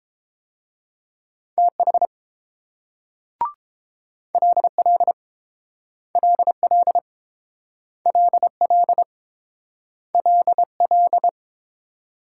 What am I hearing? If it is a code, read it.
Morse code, heard as a keyed tone, about 30 wpm, 700 Hz. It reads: TH LL LL LL LL